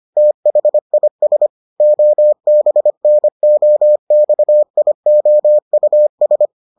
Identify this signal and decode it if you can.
Morse code, heard as a keyed tone, about 25 wpm, 600 Hz. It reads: THIS OBNOXIOUS